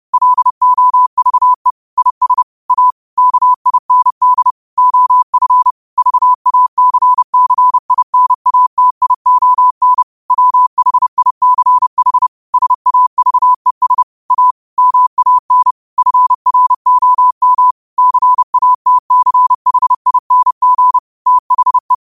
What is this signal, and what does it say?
Morse code, heard as a keyed tone, about 30 wpm, 1000 Hz. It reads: LOVE IS A KIND OF VACCINATION WHICH SAVES A MAN FROM CATCHING THE
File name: qotdfiles / 30wpm